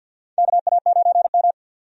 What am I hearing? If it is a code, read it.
Morse code, heard as a keyed tone, about 48 wpm, 700 Hz. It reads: KA9M
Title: USA Call Signs - 4 Characters - 50wpm